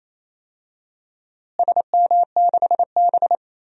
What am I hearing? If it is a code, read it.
Morse code, heard as a keyed tone, about 27 wpm, 700 Hz. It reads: SM6B
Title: DXCC-100 Call Signs - 4 Characters - Repeated 3x - 28wpm